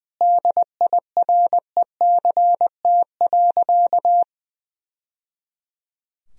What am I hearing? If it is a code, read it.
Morse code, heard as a keyed tone, about 20 wpm, 700 Hz. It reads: DIRECT.
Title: Sentences from Top 1000 Words - Review Part 2 - 20wpm